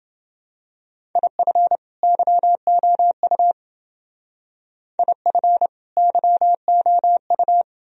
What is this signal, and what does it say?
Morse code, heard as a keyed tone, about 28 wpm, 700 Hz. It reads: IF YOU IF YOU